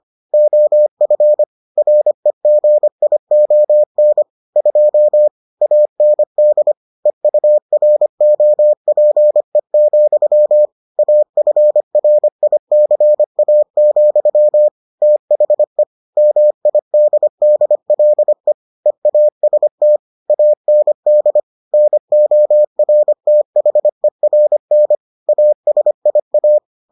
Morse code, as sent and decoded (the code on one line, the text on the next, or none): --- ..-. / .-. --. .. --- -. / ..--- / .- -. -.. / . ..- .-. --- .--. --..-- / .- ..-. .-. .. -.-. .- --..-- / - .... / -- .. -.. -.. .-.. / . .- ... - / .- -. -.. / -. --- .-. - .... .-. -. / .- ... .. .-
OF REGION 2 AND EUROPE, AFRICA, THE MIDDLE EAST AND NORTHERN ASIA